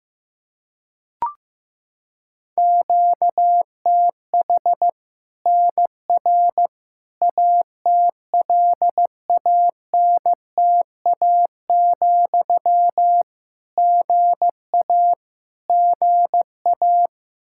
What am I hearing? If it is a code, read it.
Morse code, heard as a keyed tone, about 16 wpm, 700 Hz. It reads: QTH NR ATLANTA, GA GA
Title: K1USN SST QSO Practice 15wpm